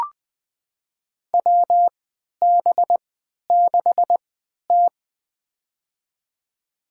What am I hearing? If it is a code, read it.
Morse code, heard as a keyed tone, about 20 wpm, spaced as for 11 wpm, 700 Hz. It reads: WB6T
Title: USA Call Signs - 3x Character Spacing - 20wpm